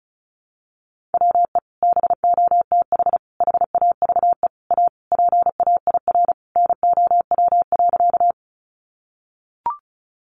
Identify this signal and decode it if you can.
Morse code, heard as a keyed tone, about 34 wpm, 700 Hz. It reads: WE BOTH HAVE A PAIR NOW.
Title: Sentences from Top 1200 Words 35wpm